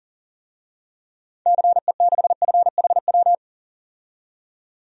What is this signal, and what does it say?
Morse code, heard as a keyed tone, about 40 wpm, 700 Hz. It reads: CE6FHW
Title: DXCC-100 Call Signs - 6 Characters - 40wpm